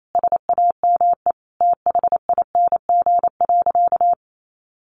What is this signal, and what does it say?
Morse code, heard as a keyed tone, about 28 wpm, 700 Hz. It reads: SAME THING.
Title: Sentences from Top 1100 Words 28wpm